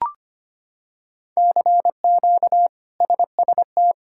Morse code, heard as a keyed tone, about 25 wpm, 700 Hz. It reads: CQ SST
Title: K1USN SST QSO Practice - 25wpm